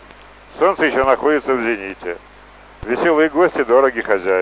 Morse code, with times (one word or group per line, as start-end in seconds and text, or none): none